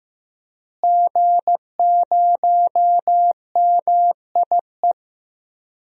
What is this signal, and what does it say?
Morse code, heard as a keyed tone, about 15 wpm, 700 Hz. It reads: G0MIE
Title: DXCC-100 Call Signs - 5 Characters - Repeated 3x - 15wpm